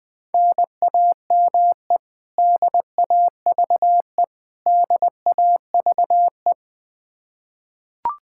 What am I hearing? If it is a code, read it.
Morse code, heard as a keyed tone, about 21 wpm, 700 Hz. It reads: NAME DAVE DAVE
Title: Practice QSO - Exchange 1 - Beginning - 20wpm